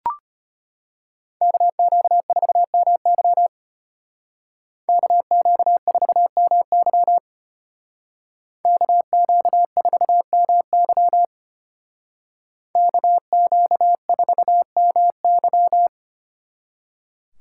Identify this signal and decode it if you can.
Morse code, heard as a keyed tone, about 33 wpm, 700 Hz. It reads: KQ4MY KQ4MY KQ4MY KQ4MY